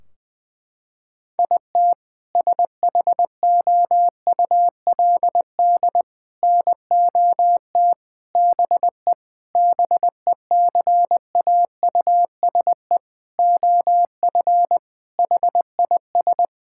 Morse code, heard as a keyed tone, about 21 wpm, 700 Hz. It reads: IT SHOULD NOT BE BECAUSE OF HIS
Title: Sentences from Top 1600 Words 20wpm